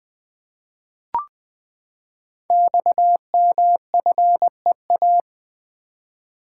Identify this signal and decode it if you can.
Morse code, heard as a keyed tone, about 20 wpm, 700 Hz. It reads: XMFEA
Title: Code Groups - 5 Letter:Numbers - 20wpm